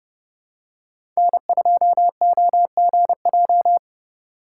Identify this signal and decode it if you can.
Morse code, heard as a keyed tone, about 30 wpm, 700 Hz. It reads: N2OGJ
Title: USA Call Signs - 5 Characters - 2x Character Spacing - 30wpm